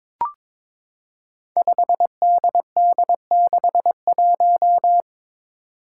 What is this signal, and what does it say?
Morse code, heard as a keyed tone, about 22 wpm, 700 Hz. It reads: 5DD61
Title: Five Challenging Letter-Number Combinations 22wpm